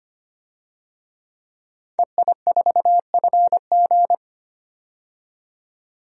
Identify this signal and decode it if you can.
Morse code, heard as a keyed tone, about 25 wpm, 690 Hz. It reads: EI4FG
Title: DXCC-100 Call Signs - 5 Characters - 25wpm